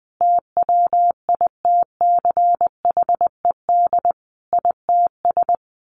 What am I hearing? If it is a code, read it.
Morse code, heard as a keyed tone, about 20 wpm, 700 Hz. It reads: TWITCHED ITS
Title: A Sound Of Thunder 20wpm